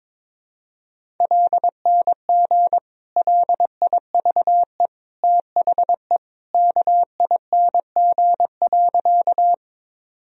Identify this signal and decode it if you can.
Morse code, heard as a keyed tone, about 22 wpm, 700 Hz. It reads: LNG LIVE THE KING.
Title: Sentences from Top 400 Words and Abbreviations 22wpm